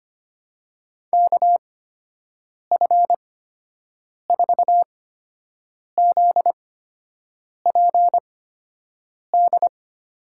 Morse code, heard as a keyed tone, about 25 wpm, spaced as for 7 wpm, 700 Hz. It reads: KF4ZPD
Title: USA Call Signs - 6 Characters - 8x Character Spacing - 25wpm